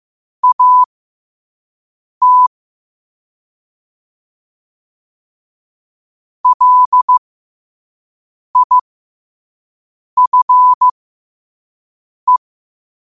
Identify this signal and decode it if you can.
Morse code, heard as a keyed tone, about 15 wpm, spaced as for 5 wpm, 1000 Hz. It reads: AT LIFE
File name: qotdfiles / 5wpm